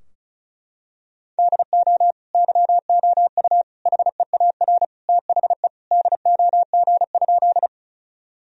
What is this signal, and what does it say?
Morse code, heard as a keyed tone, about 36 wpm, 700 Hz. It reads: DO YOU HEAR THE DOG?
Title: Sentences from Top 1000 Words - Review Part 2 - 35wpm